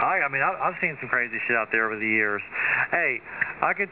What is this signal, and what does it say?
Emergency Alert System (SAME) tones, heard as harsh buzzing bursts, with speech. It is heard from 0.8 to 3.5 s.